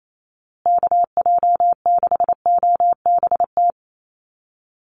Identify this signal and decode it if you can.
Morse code, heard as a keyed tone, about 28 wpm, 700 Hz. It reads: KJ6OBT
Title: USA Call Signs Repeated 3x 28wpm